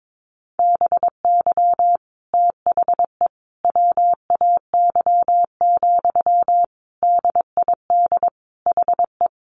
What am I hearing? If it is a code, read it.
Morse code, heard as a keyed tone, about 22 wpm, 700 Hz. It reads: BY THE WAY, DID HE
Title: Sentences from Top 1300 Words 22wpm